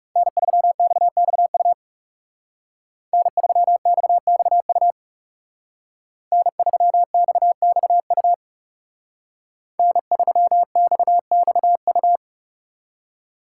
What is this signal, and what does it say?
Morse code, heard as a keyed tone, about 37 wpm, 700 Hz. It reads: N3XXU N3XXU N3XXU N3XXU
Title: Speed Racing - USA 1x3 Call Signs 30wpm